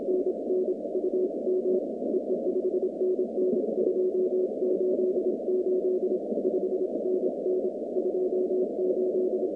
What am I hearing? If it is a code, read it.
Morse code, heard as a keyed tone, about 26 wpm, 370 Hz, with noise. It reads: NN UM EE 5NN 28 8 5NN 28
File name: 2019-cw-wpx-kt5j-k5pi